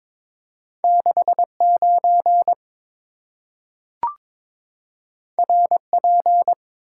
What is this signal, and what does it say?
Morse code, heard as a keyed tone, about 22 wpm, 700 Hz. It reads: 69 RP
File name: practice / 22wpm / Two Letter-Number Combinations - 22wpm